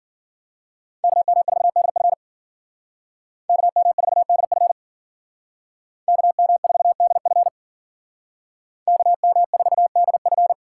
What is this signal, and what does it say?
Morse code, heard as a keyed tone, about 52 wpm, 700 Hz. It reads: KM4DF KM4DF KM4DF KM4DF